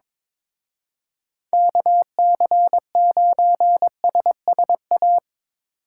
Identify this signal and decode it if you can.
Morse code, heard as a keyed tone, about 24 wpm, 700 Hz. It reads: KC9SSA
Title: Speed Racing - USA Call Signs 22wpm